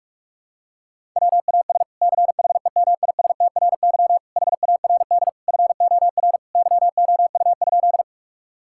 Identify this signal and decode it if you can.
Morse code, heard as a keyed tone, about 46 wpm, 700 Hz. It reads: WAS CHEMISTRY HARD FOR YOU?
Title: Sentences from Top 1000 Words - Review Part 6 - 45wpm